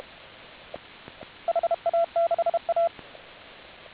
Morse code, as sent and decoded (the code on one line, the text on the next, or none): .... .- -.... .-
HA6A